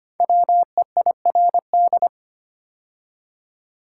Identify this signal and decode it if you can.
Morse code, heard as a keyed tone, about 25 wpm, 700 Hz. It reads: WEIRD
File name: 5 Letter Words - First 3 Letters Given - 25wpm